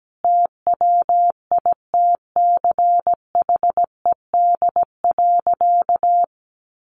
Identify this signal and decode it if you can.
Morse code, heard as a keyed tone, about 17 wpm, 700 Hz. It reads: TWITCHED.
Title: A Sound Of Thunder 17wpm